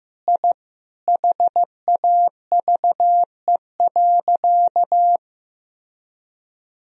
Morse code, heard as a keyed tone, about 15 wpm, 700 Hz. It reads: I HAVE.
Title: Sentences from Top 1000 Words - Review Part 1 - 15wpm